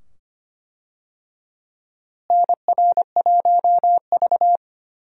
Morse code, heard as a keyed tone, about 25 wpm, 700 Hz. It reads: NR1V
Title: USA Call Signs - 4 Characters - Repeated 2x - 25wpm